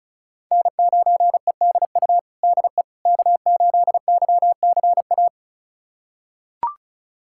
Morse code, heard as a keyed tone, about 36 wpm, 700 Hz. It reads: N9EDU DE K8YCA
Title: USA Callsign DE Callsign - 5 Characters - 35wpm